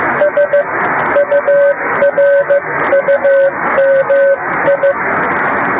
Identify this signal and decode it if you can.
Morse code, heard as a keyed tone, about 15 wpm, spaced as for 13 wpm, 560 Hz, with radio noise. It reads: SURUMI
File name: Beacon / 10GbeaconJH1UGF awasaaki